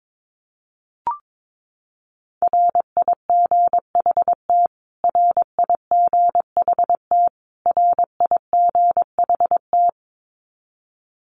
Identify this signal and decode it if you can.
Morse code, heard as a keyed tone, about 22 wpm, 700 Hz. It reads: RIGHT RIGHT RIGHT
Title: Top 200 Words Repeated 22wpm